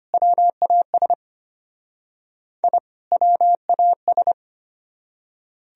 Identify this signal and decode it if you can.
Morse code, heard as a keyed tone, about 33 wpm, 700 Hz. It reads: WAS I WAS